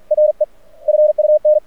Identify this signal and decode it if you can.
Morse code, heard as a keyed tone, about 46 wpm, 610 Hz. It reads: RE FUN
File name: archived / CRSnet